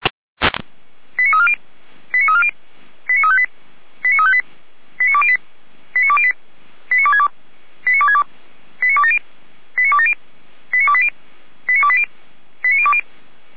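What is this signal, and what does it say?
Five-tone selective call (ZVEI), heard as quick runs of five tones, tones of about 70 ms. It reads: 89340 89340 89358 89358 89208 89208 89262 89262 89270 89270 89280 89280 89020
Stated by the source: ZVEI five-tone selective call 89333 received on an SDR.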